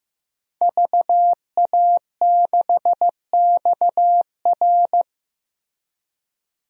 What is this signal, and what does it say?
Morse code, heard as a keyed tone, about 15 wpm, 700 Hz. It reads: VA6XR
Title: DXCC-100 Call Signs - 5 Characters - Repeated 3x - 15wpm